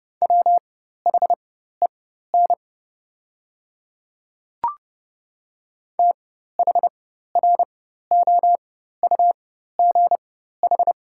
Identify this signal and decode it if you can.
Morse code, heard as a keyed tone, about 30 wpm, spaced as for 14 wpm, 700 Hz. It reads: WHEN THROUGH
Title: 4x Character Spacing - Top 200 Words - 30wpm